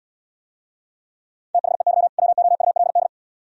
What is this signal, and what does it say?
Morse code, heard as a keyed tone, about 76 wpm, 700 Hz. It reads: THEY WOULD